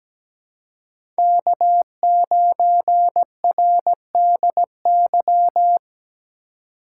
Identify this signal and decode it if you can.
Morse code, heard as a keyed tone, about 17 wpm, 700 Hz. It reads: K9RDY
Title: USA Call Signs - 3x Character Spacing - 17wpm